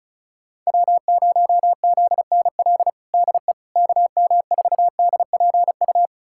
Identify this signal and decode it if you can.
Morse code, heard as a keyed tone, about 36 wpm, 690 Hz. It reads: W0ZNL DE KM4DPU
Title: USA Callsign DE Callsign 35wpm